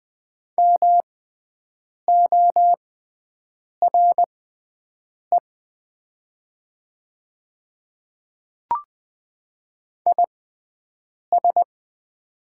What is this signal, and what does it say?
Morse code, heard as a keyed tone, about 20 wpm, spaced as for 7 wpm, 700 Hz. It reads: MORE IS
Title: 6x Character Spacing - Top 100 Words - 20wpm